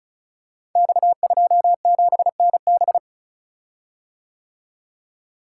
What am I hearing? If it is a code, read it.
Morse code, heard as a keyed tone, about 36 wpm, 700 Hz. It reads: X27NB
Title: Code Groups - 5 Letter:Numbers - 35wpm